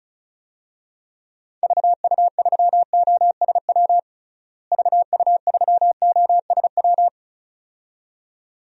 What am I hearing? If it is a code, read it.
Morse code, heard as a keyed tone, about 36 wpm, 700 Hz. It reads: VU3OSW VU3OSW
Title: DXCC-100 Call Signs - 6 Characters - Repeated 2x - 35wpm